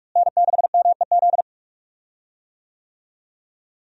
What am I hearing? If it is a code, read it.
Morse code, heard as a keyed tone, about 44 wpm, 700 Hz. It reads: N6MEZ
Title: USA Call Signs - 5 Characters - Repeated 2x - 45wpm